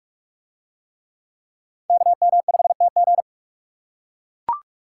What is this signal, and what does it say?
Morse code, heard as a keyed tone, about 42 wpm, 700 Hz. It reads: KM5TG